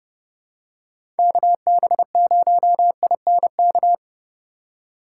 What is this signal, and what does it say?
Morse code, heard as a keyed tone, about 31 wpm, 700 Hz. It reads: KB0INK
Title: USA Call Signs - 6 Characters - 4x Character Spacing - 30wpm